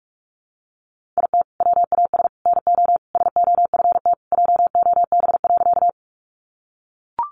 Morse code, heard as a keyed tone, about 44 wpm, 700 Hz. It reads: IT WAS NO SOFT JOB.
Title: Sentences from Top 1500 Words 45wpm